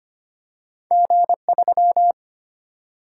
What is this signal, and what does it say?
Morse code, heard as a keyed tone, about 25 wpm, 700 Hz. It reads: G3
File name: Two Letter-Number Combinations - 25wpm